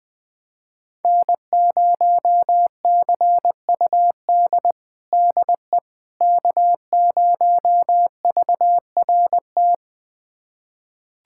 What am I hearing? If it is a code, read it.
Morse code, heard as a keyed tone, about 20 wpm, 700 Hz. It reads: N0CUD DE K0VRT
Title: USA Callsign DE Callsign - 5 Characters - 20wpm